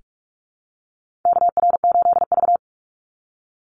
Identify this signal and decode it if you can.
Morse code, heard as a keyed tone, about 44 wpm, 700 Hz. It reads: KR8V
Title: Call Signs - 45wpm